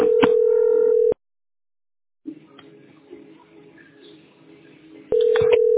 Ringback tone (North America), 440 + 480 Hz.